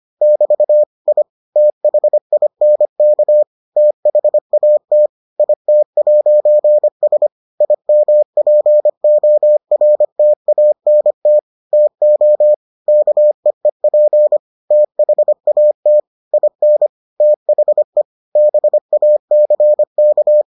Morse code, heard as a keyed tone, about 25 wpm, 600 Hz. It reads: <BT> I THINK THAT IT'S IMPORTANT TO KEEP THAT IN THE BACK